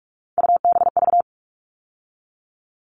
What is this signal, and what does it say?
Morse code, heard as a keyed tone, about 44 wpm, 700 Hz. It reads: UBV
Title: All Cut Numbers - 3-Digit 45wpm